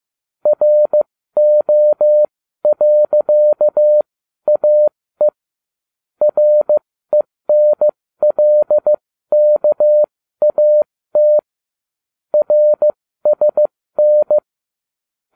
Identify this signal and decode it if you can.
Morse code, heard as a keyed tone, about 15 wpm, spaced as for 13 wpm, 600 Hz. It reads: RO.AE RENLKAT RSN